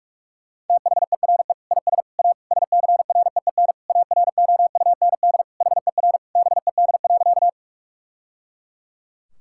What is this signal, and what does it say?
Morse code, heard as a keyed tone, about 46 wpm, 700 Hz. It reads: THERE IS A SCREEN AROUND HER BED.
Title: Sentences from Top 1000 Words - Review Part 6 - 45wpm